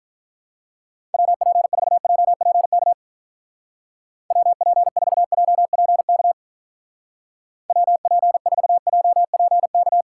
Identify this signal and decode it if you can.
Morse code, heard as a keyed tone, about 42 wpm, 700 Hz. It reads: WP4JPK WP4JPK WP4JPK
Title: Speed Racing - USA 2x3 Call Signs 35wpm